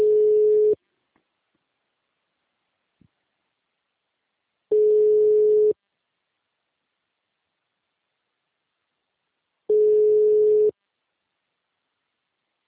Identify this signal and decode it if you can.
Ringback tone (Europe, 425 Hz).
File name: out- -2028- - - .14542